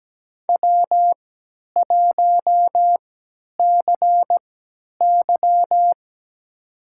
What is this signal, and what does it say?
Morse code, heard as a keyed tone, about 17 wpm, spaced as for 10 wpm, 700 Hz. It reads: W1CY
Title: USA Call Signs - 4 Characters - 3x Character Spacing - 17wpm